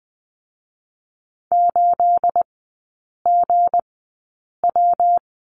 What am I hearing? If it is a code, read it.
Morse code, heard as a keyed tone, about 20 wpm, spaced as for 8 wpm, 690 Hz. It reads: 8GW
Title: 212 Lesson 34e - Number 8 - Three Characters 20wpm